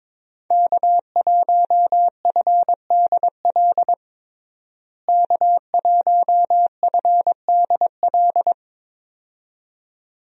Morse code, heard as a keyed tone, about 22 wpm, 700 Hz. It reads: K1FDL K1FDL
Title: USA Call Signs - 5 Characters - Repeated 2x - 22wpm